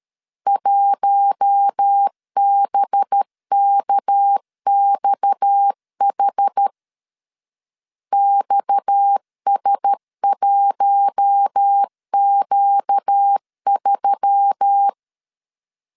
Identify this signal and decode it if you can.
Morse code, heard as a keyed tone, about 13 wpm, 800 Hz. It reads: 1BKXH XS1Q3
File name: LEZIONI / 6 PKX0 / 6 PKX 12wpm 2024